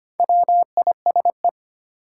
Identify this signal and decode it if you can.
Morse code, heard as a keyed tone, about 25 wpm, 700 Hz. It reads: WISE
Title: Sentences from Top 1600 Words - New Word Drill - 25wpm